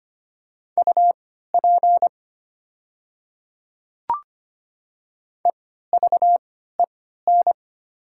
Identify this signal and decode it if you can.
Morse code, heard as a keyed tone, about 25 wpm, spaced as for 14 wpm, 700 Hz. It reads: UP EVEN
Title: 3x Character Spacing - Top 500 Words - 25wpm